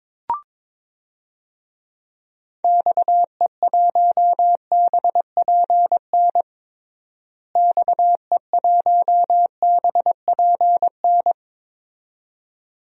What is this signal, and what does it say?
Morse code, heard as a keyed tone, about 22 wpm, 700 Hz. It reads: XE1BPN XE1BPN
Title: DXCC-100 Call Signs - 6 Characters - Repeated 2x - 22wpm